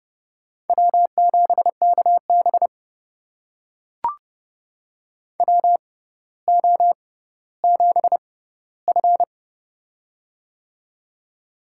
Morse code, heard as a keyed tone, about 30 wpm, spaced as for 15 wpm, 700 Hz. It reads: W7KB WO7F
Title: USA Call Signs - 4 Characters - 6x Character Spacing - 30wpm